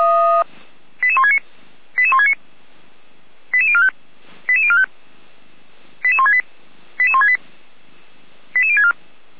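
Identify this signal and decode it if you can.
Five-tone selective call (ZVEI), heard as quick runs of five tones, tones of about 70 ms. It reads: 88169 88169 88045 88045 88168 88168 88064
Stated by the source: Five-tone ZVEI selective call 88551, received by SDR.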